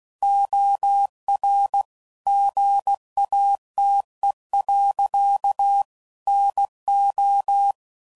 Morse code, heard as a keyed tone, about 16 wpm, 800 Hz. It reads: OR GATE. NO
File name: CW Code Groups @15WPM / Technical Subjects / OR Gate